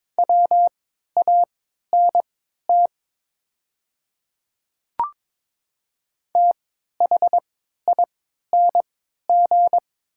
Morse code, heard as a keyed tone, about 22 wpm, spaced as for 12 wpm, 690 Hz. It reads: WANT THING
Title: 3x Character Spacing - Top 200 Words - 22wpm